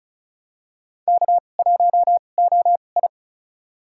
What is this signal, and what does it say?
Morse code, heard as a keyed tone, about 36 wpm, spaced as for 25 wpm, 700 Hz. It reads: K1OI
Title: USA Call Signs - 2x Character Spacing - 35wpm